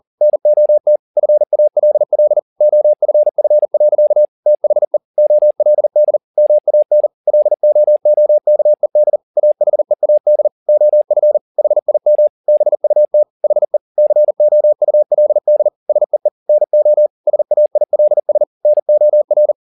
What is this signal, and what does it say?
Morse code, heard as a keyed tone, about 40 wpm, 600 Hz. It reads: NOT FALL OFF. THE OLD MAN LOOKED AHEAD OF HIM BUT HE COULD SEE NO SAILS NOR
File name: the old man and the sea 2 40wpm